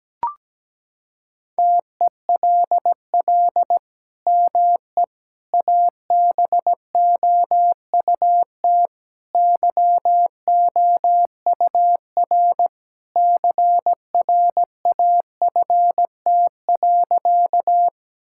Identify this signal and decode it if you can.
Morse code, heard as a keyed tone, about 17 wpm, 700 Hz. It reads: TELL ME ABOUT YOUR CRAFT.
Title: Sentences from Top 1000 Words - Review Part 6 - 17wpm